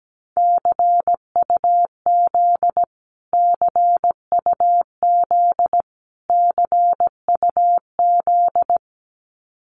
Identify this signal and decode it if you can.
Morse code, heard as a keyed tone, about 17 wpm, 700 Hz. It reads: CUZ CUZ CUZ
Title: Abbreviations Repeated 3x - 17wpm